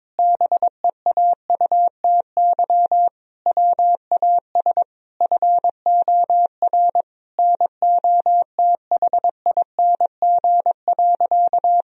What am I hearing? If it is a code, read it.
Morse code, heard as a keyed tone, about 21 wpm, 700 Hz. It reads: BEAUTY WAS FOR NOTHING.
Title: Sentences from Top 1000 Words - Review Part 2 - 22wpm